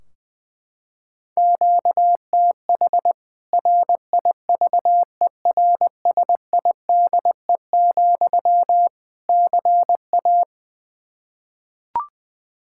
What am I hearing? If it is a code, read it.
Morse code, heard as a keyed tone, about 20 wpm, 700 Hz. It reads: QTH RIVERSIDE, CA
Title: Practice QSO - Exchange 1 - Beginning - 20wpm